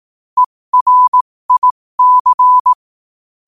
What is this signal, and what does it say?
Morse code, heard as a keyed tone, about 18 wpm, spaced as for 15 wpm, 1000 Hz. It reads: ERIC
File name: qotdfiles / 15wpm